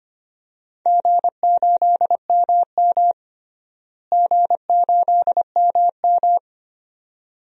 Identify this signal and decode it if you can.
Morse code, heard as a keyed tone, about 25 wpm, 700 Hz. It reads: G8MM G8MM
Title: DXCC-100 Call Signs - 4 Characters - Repeated 2x - 25wpm